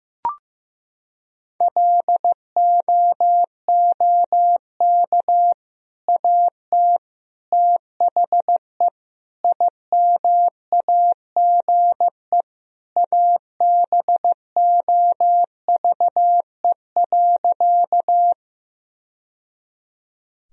Morse code, heard as a keyed tone, about 16 wpm, 700 Hz. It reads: LOOK AT THE IMAGE ABOVE.
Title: Sentences from Top 1000 Words - Review Part 4 - 15wpm